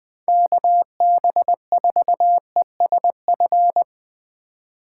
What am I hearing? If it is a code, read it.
Morse code, heard as a keyed tone, about 20 wpm, 700 Hz. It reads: KB4ESF
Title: Speed Racing - USA 2x3 Call Signs 17wpm